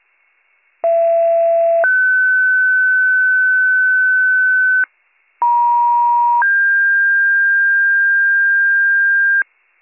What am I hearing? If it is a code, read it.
Two-tone page, heard as a one-second tone, then a three-second tone, about 670 then 1500 Hz, about 950 then 1600 Hz. It is heard from 0.8 to 9.4 s.